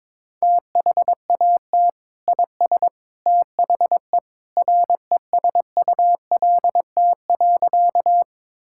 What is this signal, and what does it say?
Morse code, heard as a keyed tone, about 22 wpm, 700 Hz. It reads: THAT IS THE RESULT.
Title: Sentences from Top 1000 Words - Review Part 5 - 22wpm